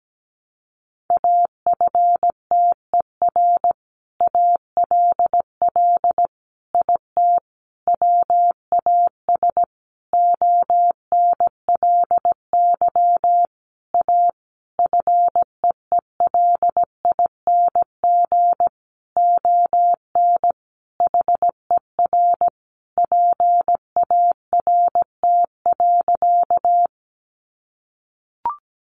Morse code, heard as a keyed tone, about 17 wpm, 700 Hz. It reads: AFTER ALL IT WAS ONLY A FEELING ON HER PART.
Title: Sentences from Top 1200 Words 17wpm